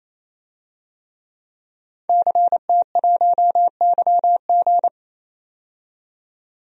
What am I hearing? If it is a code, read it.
Morse code, heard as a keyed tone, about 28 wpm, 700 Hz. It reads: CT1YG
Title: DXCC-100 Call Signs - 5 Characters - 28wpm